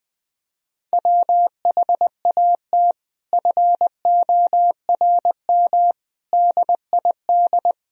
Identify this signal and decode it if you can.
Morse code, heard as a keyed tone, about 20 wpm, 700 Hz. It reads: WHAT FORM DID
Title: Sentences from Top 1000 Words - Review Part 1 - 20wpm